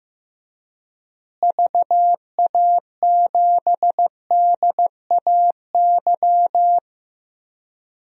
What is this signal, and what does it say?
Morse code, heard as a keyed tone, about 15 wpm, 700 Hz. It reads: VA7DAY
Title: DXCC-100 Call Signs - 6 Characters - Repeated 3x - 15wpm